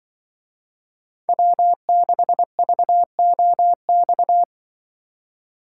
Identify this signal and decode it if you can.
Morse code, heard as a keyed tone, about 24 wpm, 700 Hz. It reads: W6VOX
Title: Speed Racing - USA 1x3 Call Signs 20wpm